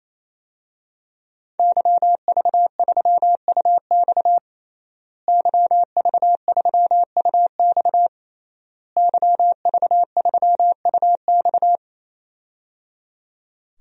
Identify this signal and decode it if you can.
Morse code, heard as a keyed tone, about 28 wpm, 700 Hz. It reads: YV3UX YV3UX YV3UX
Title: DXCC-100 Call Signs - 5 Characters - Repeated 3x - 28wpm